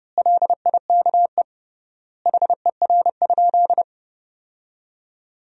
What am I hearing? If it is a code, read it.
Morse code, heard as a keyed tone, about 32 wpm, 700 Hz. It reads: LIKE HER?
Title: 3x Word-Spacing - Sentences from Top 100 Words - 30wpm